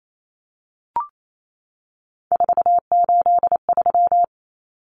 Morse code, heard as a keyed tone, about 28 wpm, 700 Hz. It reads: 483
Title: Numbers - 3-Digit 28wpm